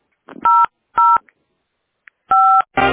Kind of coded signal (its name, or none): DTMF